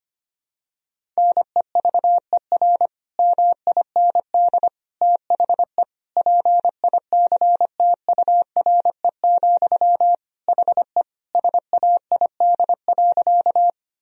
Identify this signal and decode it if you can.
Morse code, heard as a keyed tone, about 25 wpm, 700 Hz. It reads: NEVER MIND THE PICTURE, HE SAID.
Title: Sentences from Top 1000 Words - Review Part 1 - 25wpm